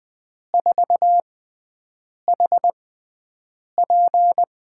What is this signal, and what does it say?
Morse code, heard as a keyed tone, about 20 wpm, spaced as for 7 wpm, 700 Hz. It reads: 4HP
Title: USA Call Signs - 6 Characters - 6x Character Spacing - 20wpm